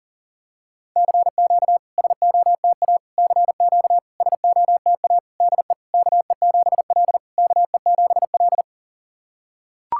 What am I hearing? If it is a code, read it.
Morse code, heard as a keyed tone, about 40 wpm, 700 Hz. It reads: CQ SOTA CQ SOTA DE KE7L KE7L
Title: SOTA QSO Practice 40wpm